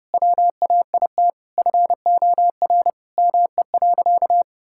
Morse code, heard as a keyed tone, about 32 wpm, 700 Hz. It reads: WAIT FOR ME.
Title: Sentences from Top 1600 Words 30wpm